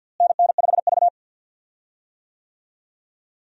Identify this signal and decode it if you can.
Morse code, heard as a keyed tone, about 51 wpm, 700 Hz. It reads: NN5V
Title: USA Call Signs - 4 Characters - Repeated 3x - 50wpm